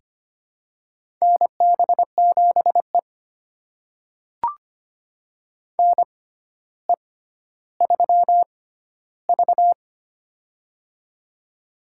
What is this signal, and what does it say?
Morse code, heard as a keyed tone, about 25 wpm, spaced as for 13 wpm, 700 Hz. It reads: NB7E NE3V
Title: USA Call Signs - 4 Characters - 6x Character Spacing - 25wpm